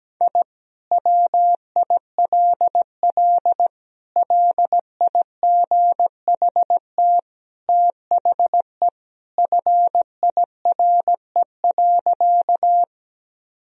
Morse code, heard as a keyed tone, about 17 wpm, 700 Hz. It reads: I WILL LIGHT THE FIRE.
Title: Sentences from Top 1000 Words - Review Part 1 - 17wpm